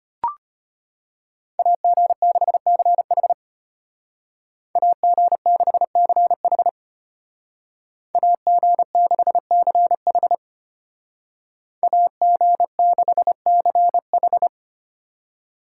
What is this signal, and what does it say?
Morse code, heard as a keyed tone, about 30 wpm, 700 Hz. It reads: AG6CH AG6CH AG6CH AG6CH
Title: Speed Racing - USA Call Signs 25wpm